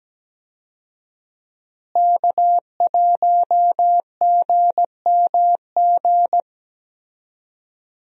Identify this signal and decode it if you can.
Morse code, heard as a keyed tone, about 17 wpm, 700 Hz. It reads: K1GMG
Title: USA Call Signs - 5 Characters - Repeated 2x - 17wpm